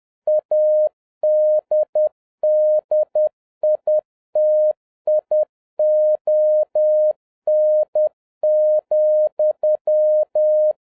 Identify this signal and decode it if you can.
Morse code, heard as a keyed tone, about 10 wpm, 610 Hz. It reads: ADDITION,